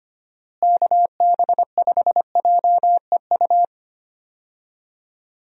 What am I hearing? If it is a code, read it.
Morse code, heard as a keyed tone, about 25 wpm, 700 Hz. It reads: KB5JEU
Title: USA Call Signs - 6 Characters - Repeated 2x - 25wpm